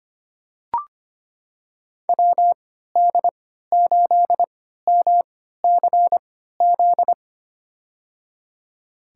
Morse code, heard as a keyed tone, about 25 wpm, spaced as for 14 wpm, 700 Hz. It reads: WD8MCZ
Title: USA Call Signs - 6 Characters - 3x Character Spacing - 25wpm